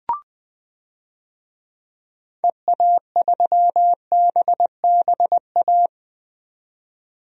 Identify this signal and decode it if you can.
Morse code, heard as a keyed tone, about 21 wpm, 700 Hz. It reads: EA3BBA
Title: DXCC-100 Call Signs - 6 Characters - 20wpm